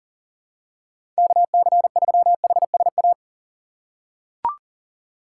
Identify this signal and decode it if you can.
Morse code, heard as a keyed tone, about 40 wpm, 700 Hz. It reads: KC3HSA
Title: USA Call Signs - 6 Characters - 4x Character Spacing - 40wpm